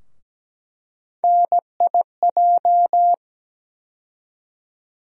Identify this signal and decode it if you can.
Morse code, heard as a keyed tone, about 17 wpm, 700 Hz. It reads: NIJ
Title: USA Call Sign Postfixes - 3 Letters 17wpm